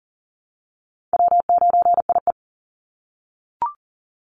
Morse code, heard as a keyed tone, about 40 wpm, 700 Hz. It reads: W9IE
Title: Easy Call Signs - 40wpm